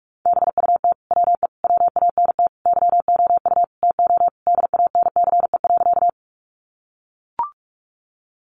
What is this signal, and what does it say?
Morse code, heard as a keyed tone, about 44 wpm, 700 Hz. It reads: BUT WE WANT YOU TO DANCE.
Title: Sentences from Top 700 Words 45wpm